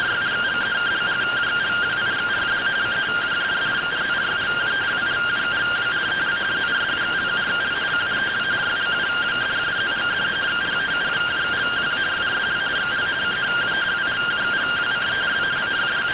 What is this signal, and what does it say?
Two-tone teleprinter signal, about 170 Hz shift.
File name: findings / AXIAL fr45 3mm.gen-35 rtty rx